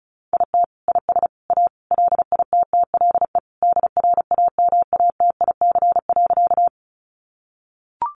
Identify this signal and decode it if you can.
Morse code, heard as a keyed tone, about 34 wpm, 700 Hz. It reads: IT IS A LITTLE DRAMATIC.
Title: Sentences from Top 1300 Words 35wpm